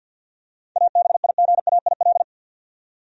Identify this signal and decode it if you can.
Morse code, heard as a keyed tone, about 48 wpm, 700 Hz. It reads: ABIGAIL
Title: USA Common Names - 50wpm